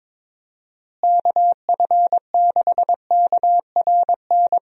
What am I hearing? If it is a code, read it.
Morse code, heard as a keyed tone, about 22 wpm, 700 Hz. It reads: KF6KRN
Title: USA Call Signs - 6 Characters - Repeated 2x - 22wpm